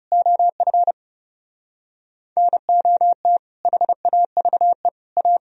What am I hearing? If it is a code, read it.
Morse code, heard as a keyed tone, about 37 wpm, 700 Hz. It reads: OF NOT HAVE A
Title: Speed-Racing - Sets of 5 Words 30wpm